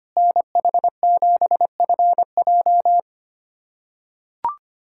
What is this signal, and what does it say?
Morse code, heard as a keyed tone, about 25 wpm, 700 Hz. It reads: NH7FJ
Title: USA Callsign DE Callsign 25wpm